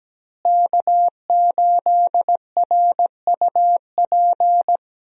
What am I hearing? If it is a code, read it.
Morse code, heard as a keyed tone, about 17 wpm, 700 Hz. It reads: K8RUP
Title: USA Call Signs - 5 Characters - Repeated 3x - 17wpm